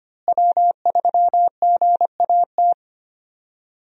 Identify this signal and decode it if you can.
Morse code, heard as a keyed tone, about 25 wpm, 700 Hz. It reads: W3GAT
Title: USA Call Signs - 5 Characters - Repeated 3x - 25wpm